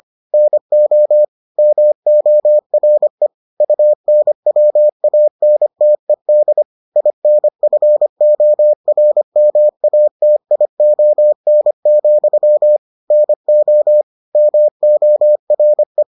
Morse code, heard as a keyed tone, about 25 wpm, 600 Hz. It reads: NO MORE UNWANTED INFORMATION, NO MORE